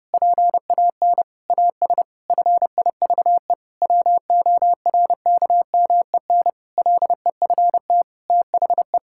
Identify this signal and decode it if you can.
Morse code, heard as a keyed tone, about 30 wpm, 700 Hz. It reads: PAN AS FIVE WORKMEN LEFT THE
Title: Pangram Challenge v2 - 30wpm